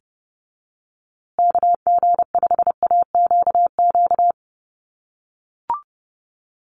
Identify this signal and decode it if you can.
Morse code, heard as a keyed tone, about 30 wpm, 700 Hz. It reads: KG5AQQ
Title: USA Call Signs Repeated 3x 30wpm